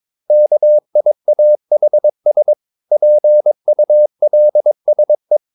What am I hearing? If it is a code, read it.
Morse code, heard as a keyed tone, about 22 wpm, 600 Hz. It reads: KIAHS PULSE
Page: Romanen Never af Ken Follet på engelsk.